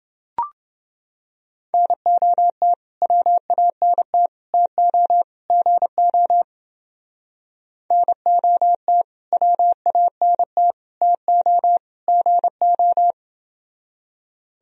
Morse code, heard as a keyed tone, about 31 wpm, 700 Hz. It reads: NOT WANT TO GO NOT WANT TO GO